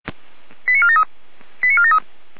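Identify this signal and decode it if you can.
Five-tone selective call (ZVEI), heard as quick runs of five tones, tones of about 70 ms. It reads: 89473 89473
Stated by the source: SDR reception of ZVEI five-tone call 89315, from the feuerwehr-orlamuende.org fire brigade server.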